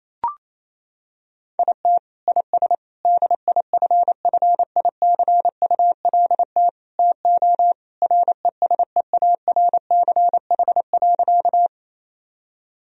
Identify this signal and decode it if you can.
Morse code, heard as a keyed tone, about 28 wpm, 700 Hz. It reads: IT IS DIFFICULT TO RESEARCH.
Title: Sentences from Top 1000 Words - Review Part 4 - 28wpm